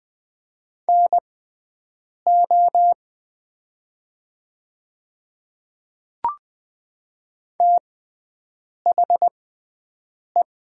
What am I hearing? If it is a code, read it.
Morse code, heard as a keyed tone, about 20 wpm, spaced as for 7 wpm, 700 Hz. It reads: NO THE
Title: 6x Character Spacing - Top 100 Words - 20wpm